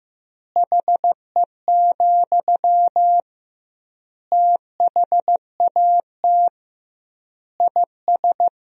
Morse code, heard as a keyed tone, about 15 wpm, 700 Hz. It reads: HE, THAT IS
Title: 2x Word-Spacing - Sentences from Top 500 Words - 15wpm